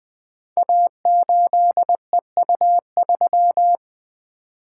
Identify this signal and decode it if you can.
Morse code, heard as a keyed tone, about 20 wpm, 700 Hz. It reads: A8EU3
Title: Code Groups - 5 Letter:Numbers - 20wpm